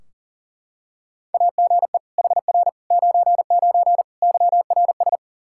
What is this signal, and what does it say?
Morse code, heard as a keyed tone, about 40 wpm, 700 Hz. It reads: AGE HR 99 YRS
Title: Practice QSO - Exchange 3 - More Info - 40wpm